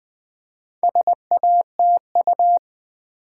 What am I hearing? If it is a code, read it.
Morse code, heard as a keyed tone, about 20 wpm, 700 Hz. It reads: SATU
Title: Build A Word - 2 Letters At A Time - 8-Letter Words - 20wpm